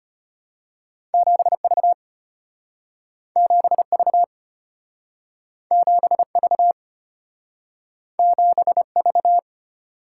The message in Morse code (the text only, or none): 7V 7V 7V 7V